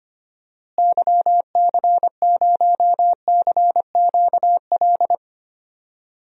Morse code, heard as a keyed tone, about 25 wpm, 700 Hz. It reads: YC0CQL
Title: DXCC-100 Call Signs - 6 Characters - Repeated 3x - 25wpm